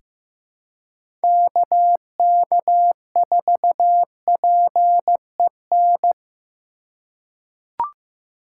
Morse code, heard as a keyed tone, about 15 wpm, 700 Hz. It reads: KK4PEN
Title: USA Call Signs - 6 Characters - 2x Character Spacing - 15wpm